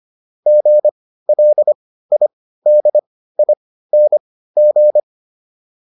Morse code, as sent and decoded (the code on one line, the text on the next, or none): --. .-.. .. -.. .. -. --.
GLIDING